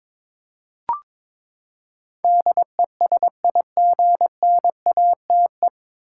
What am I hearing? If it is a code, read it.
Morse code, heard as a keyed tone, about 22 wpm, 700 Hz. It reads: DESIGNATE
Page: Morse Code Ninja practice 22wpm